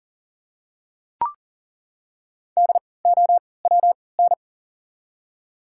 Morse code, heard as a keyed tone, about 40 wpm, spaced as for 23 wpm, 700 Hz. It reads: DOWN
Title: 3x Character Spacing - Top 200 Words - 40wpm